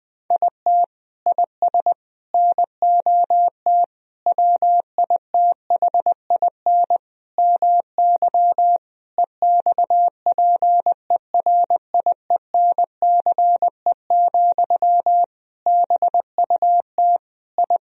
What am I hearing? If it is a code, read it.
Morse code, heard as a keyed tone, about 20 wpm, 700 Hz. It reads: IT IS NOT WITHIN MY EXPERIENCE, BUT I
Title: Sentences from Top 1000 Words - Review Part 4 - 20wpm